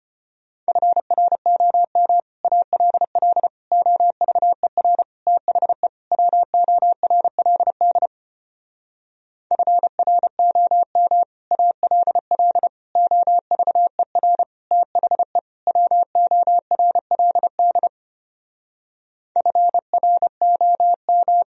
Morse code, heard as a keyed tone, about 31 wpm, 700 Hz. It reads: FROM ALL OVER THE WORLD FROM ALL OVER THE WORLD FROM
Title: Speed-Racing - Sets of 5 Words 25wpm